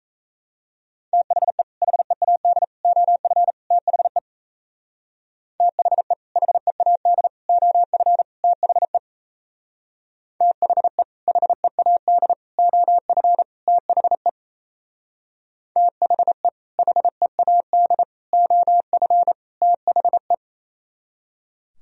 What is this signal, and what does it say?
Morse code, heard as a keyed tone, about 36 wpm, 700 Hz. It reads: THE HEAD OF THE THE HEAD OF THE THE HEAD OF THE THE HEAD OF THE